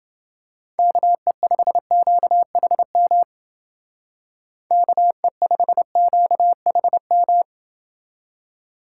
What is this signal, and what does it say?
Morse code, heard as a keyed tone, about 29 wpm, 700 Hz. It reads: KE5QHM KE5QHM